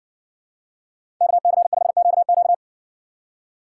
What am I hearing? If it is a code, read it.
Morse code, heard as a keyed tone, about 63 wpm, 700 Hz. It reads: KC5QQ